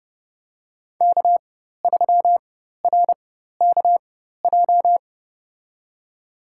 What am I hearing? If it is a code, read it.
Morse code, heard as a keyed tone, about 30 wpm, spaced as for 14 wpm, 700 Hz. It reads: K3RKJ
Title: USA Call Signs - 5 Characters - 4x Character Spacing - 30wpm